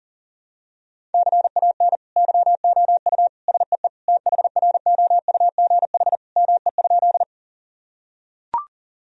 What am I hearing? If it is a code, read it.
Morse code, heard as a keyed tone, about 40 wpm, 700 Hz. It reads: CAN YOU SEE THROUGH ME?
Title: Sentences from Top 1000 Words - Review Part 1 - 40wpm